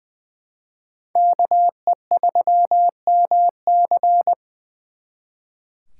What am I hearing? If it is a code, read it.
Morse code, heard as a keyed tone, about 23 wpm, 700 Hz. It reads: KE3MC
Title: Speed Racing - USA 2x2 Call Signs 20wpm